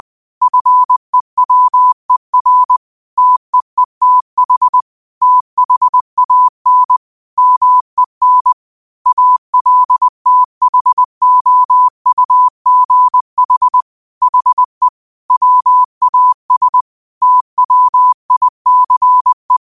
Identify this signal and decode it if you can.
Morse code, heard as a keyed tone, about 20 wpm, 1000 Hz. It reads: FEWER TEETH THAN MEN ALTHOUGH HE WAS TWICE
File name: qotdfiles / 20wpm